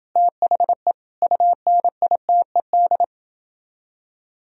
Morse code, heard as a keyed tone, about 29 wpm, 700 Hz. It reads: THE UNITED